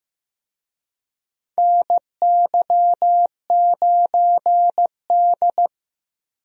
Morse code, heard as a keyed tone, about 15 wpm, 700 Hz. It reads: NY9D
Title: USA Call Signs - 4 Characters - Repeated 2x - 15wpm